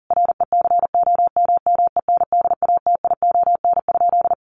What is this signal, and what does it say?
Morse code, heard as a keyed tone, about 40 wpm, 700 Hz. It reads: RECOMMENDATION?
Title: Sentences from Top 1400 Words 40wpm